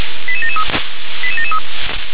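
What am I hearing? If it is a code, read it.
Five-tone selective call (ZVEI), heard as quick runs of five tones, tones of about 70 ms. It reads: 88773 88773
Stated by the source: ZVEI five-tone selective call received by SDR.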